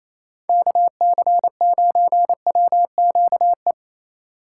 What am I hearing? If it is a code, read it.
Morse code, heard as a keyed tone, about 34 wpm, 700 Hz. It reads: KC9WQE
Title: Speed Racing - USA 2x3 Call Signs 28wpm